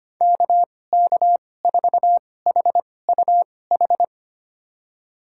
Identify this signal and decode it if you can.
Morse code, heard as a keyed tone, about 25 wpm, spaced as for 18 wpm, 700 Hz. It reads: KK4HUH
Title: USA Call Signs - 2x Character Spacing - 25wpm